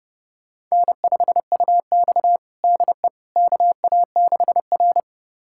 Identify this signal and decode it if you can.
Morse code, heard as a keyed tone, about 30 wpm, 700 Hz. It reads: N5UX DE KA6R
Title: USA Callsign DE Callsign - 4 Characters - 30wpm